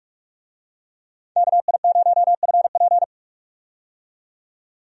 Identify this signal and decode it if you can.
Morse code, heard as a keyed tone, about 46 wpm, 700 Hz. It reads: KI0FP
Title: USA Call Signs - 5 Characters - 45wpm